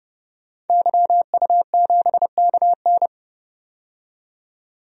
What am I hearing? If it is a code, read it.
Morse code, heard as a keyed tone, about 31 wpm, 700 Hz. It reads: YU7KN
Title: DXCC-100 Call Signs - 5 Characters - 30wpm